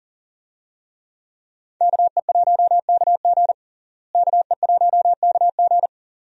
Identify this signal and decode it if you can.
Morse code, heard as a keyed tone, about 37 wpm, 700 Hz. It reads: KE1KG KE1KG
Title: USA Call Signs - 5 Characters - Repeated 2x - 40wpm